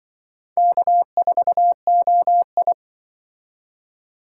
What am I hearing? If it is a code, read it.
Morse code, heard as a keyed tone, about 24 wpm, 700 Hz. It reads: K4OI
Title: Speed Racing - USA 1x2 Call Signs 20wpm